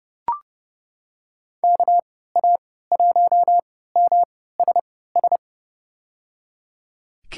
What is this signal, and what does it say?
Morse code, heard as a keyed tone, about 30 wpm, spaced as for 17 wpm, 700 Hz. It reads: KA1MSS
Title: USA Call Signs - 6 Characters - 3x Character Spacing - 30wpm